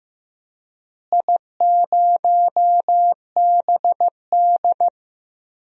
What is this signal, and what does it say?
Morse code, heard as a keyed tone, about 15 wpm, 700 Hz. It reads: I0BD
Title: DXCC-100 Call Signs - 4 Characters - Repeated 2x - 15wpm